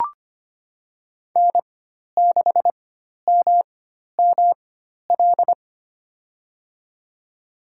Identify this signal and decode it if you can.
Morse code, heard as a keyed tone, about 25 wpm, spaced as for 12 wpm, 700 Hz. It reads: N6MML
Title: USA Call Signs - 5 Characters - 4x Character Spacing - 25wpm